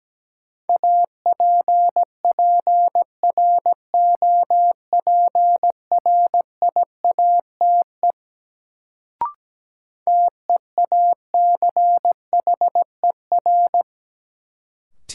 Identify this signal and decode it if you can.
Morse code, heard as a keyed tone, about 17 wpm, 700 Hz. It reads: APPROPRIATE TEACHER
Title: Sentences from Top 900 Words - New Word Drill - 17wpm